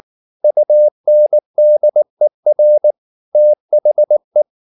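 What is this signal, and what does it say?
Morse code, heard as a keyed tone, about 19 wpm, 600 Hz. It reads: UNDER THE
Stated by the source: Morse code at 19 wpm from CW med Gnister, reading Ken Follett's novel Never in English.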